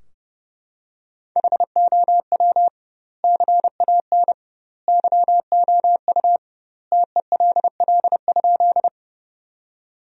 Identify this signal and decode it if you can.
Morse code, heard as a keyed tone, about 32 wpm, 700 Hz. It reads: HOW CAN YOU TELL?
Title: 2x Word-Spacing - Sentences from Top 200 Words - 30wpm